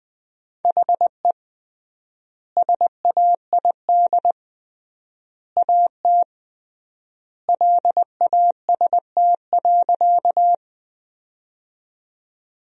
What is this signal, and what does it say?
Morse code, heard as a keyed tone, about 20 wpm, 700 Hz. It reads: HE SAID AT LAST.
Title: 3x Word-Spacing - Sentences from Top 300 Words - 20wpm